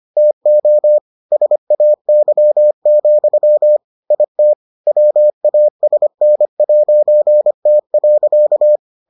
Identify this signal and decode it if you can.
Morse code, heard as a keyed tone, about 25 wpm, 600 Hz. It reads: TO SAY, IT WASN'T.